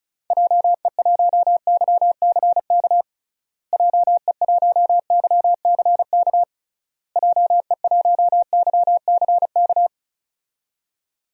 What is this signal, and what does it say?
Morse code, heard as a keyed tone, about 36 wpm, 700 Hz. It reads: JE1YCK JE1YCK JE1YCK
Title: DXCC-100 Call Signs - 6 Characters - Repeated 3x - 35wpm